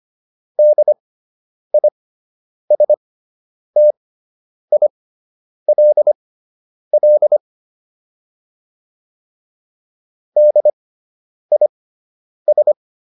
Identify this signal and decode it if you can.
Morse code, heard as a keyed tone, about 25 wpm, spaced as for 9 wpm, 600 Hz. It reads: DISTILL DIS